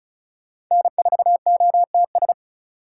Morse code, heard as a keyed tone, about 43 wpm, 700 Hz. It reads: N4OTS
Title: Speed Racing - USA 1x3 Call Signs 35wpm